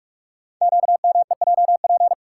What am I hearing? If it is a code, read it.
Morse code, heard as a keyed tone, about 44 wpm, 700 Hz. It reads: QMEJP
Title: Code Groups - 5 Letter:Numbers - 45wpm